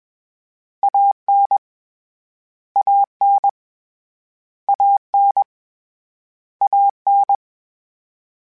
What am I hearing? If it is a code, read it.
Morse code, heard as a keyed tone, about 21 wpm, 800 Hz. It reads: AN AN AN AN
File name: Short Runs / Short Runs-mp3s / ShortContrasts @20